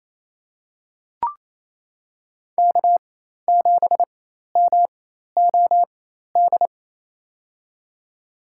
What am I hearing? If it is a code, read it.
Morse code, heard as a keyed tone, about 28 wpm, spaced as for 13 wpm, 700 Hz. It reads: K7MOD
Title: USA Call Signs - 4x Character Spacing - 28wpm